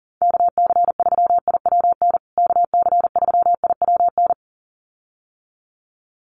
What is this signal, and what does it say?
Morse code, heard as a keyed tone, about 40 wpm, 700 Hz. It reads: KC3IWN KC3IWN
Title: USA Call Signs Repeated 2x 40wpm